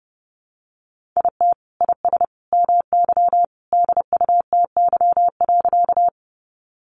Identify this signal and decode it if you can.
Morse code, heard as a keyed tone, about 30 wpm, 700 Hz. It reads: IT IS MY DUTY.
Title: Sentences from Top 1500 Words 30wpm